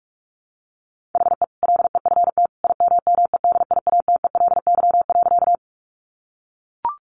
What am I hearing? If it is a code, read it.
Morse code, heard as a keyed tone, about 44 wpm, 700 Hz. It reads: HE LEFT IMMEDIATELY.
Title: Sentences from Top 1100 Words 45wpm